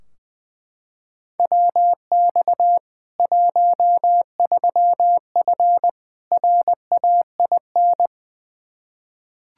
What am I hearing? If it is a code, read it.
Morse code, heard as a keyed tone, about 20 wpm, 700 Hz. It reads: WX 13F RAIN
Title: Practice QSO - Exchange 2 - Info - 20wpm